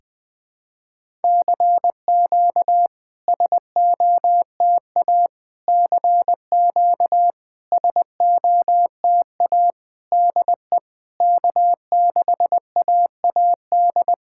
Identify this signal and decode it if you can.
Morse code, heard as a keyed tone, about 20 wpm, 690 Hz. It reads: CQ SOTA CQ SOTA DE K6AAD
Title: SOTA QSO Practice 20wpm